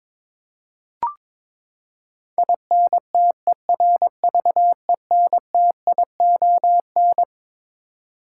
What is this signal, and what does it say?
Morse code, heard as a keyed tone, about 22 wpm, 700 Hz. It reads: INTERVENTION
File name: Twelve Letter Words - 22wpm